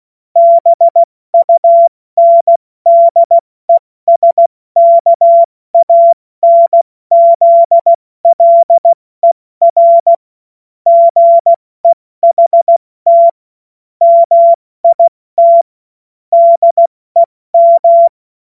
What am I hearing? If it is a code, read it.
Morse code, heard as a keyed tone, about 16 wpm, spaced as for 14 wpm, 680 Hz. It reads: BUNDESKANZLER GEHT MIT DEM
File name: Witz-30-September-2025-70-F80